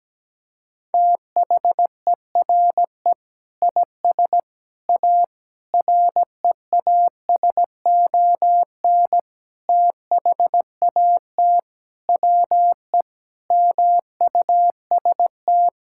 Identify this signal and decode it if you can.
Morse code, heard as a keyed tone, about 17 wpm, 700 Hz. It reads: THERE IS A REASON THAT WE MUST
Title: Sentences from Top 1000 Words - Review Part 5 - 17wpm